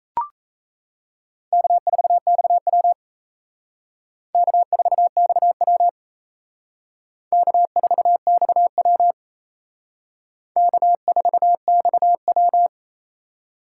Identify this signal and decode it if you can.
Morse code, heard as a keyed tone, about 35 wpm, 700 Hz. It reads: K4XW K4XW K4XW K4XW